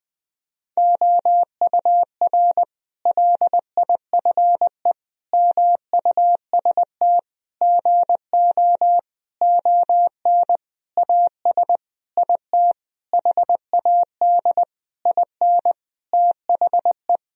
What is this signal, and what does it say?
Morse code, heard as a keyed tone, about 20 wpm, 700 Hz. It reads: OUR LIFE MUST GO ON AS IT HAD IN THE
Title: Sentences from Top 1000 Words - Review Part 3 - 20wpm